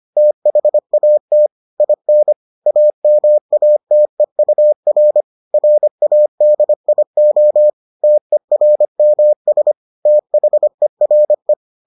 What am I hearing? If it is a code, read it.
Morse code, heard as a keyed tone, about 25 wpm, 600 Hz. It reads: THAT IN AMATEUR RADIO TERMS THERE